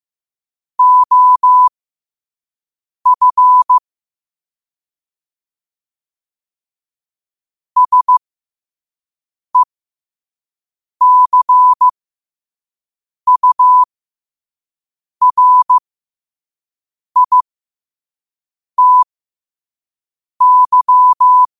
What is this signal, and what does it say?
Morse code, heard as a keyed tone, about 15 wpm, spaced as for 5 wpm, 1000 Hz. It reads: OF SECURITY
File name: qotdfiles / 5wpm